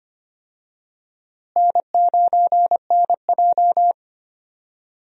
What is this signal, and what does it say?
Morse code, heard as a keyed tone, about 25 wpm, 700 Hz. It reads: N9NJ
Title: USA Call Signs - 4 Characters - Repeated 2x - 25wpm